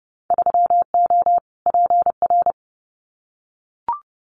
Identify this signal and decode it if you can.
Morse code, heard as a keyed tone, about 31 wpm, 700 Hz. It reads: 3O PR
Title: Winter Field Day Exchanges Encore 30wpm